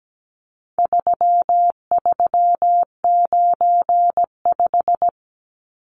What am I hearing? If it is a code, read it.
Morse code, heard as a keyed tone, about 17 wpm, 700 Hz. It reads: 3395
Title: Numbers - 4-Digit 17wpm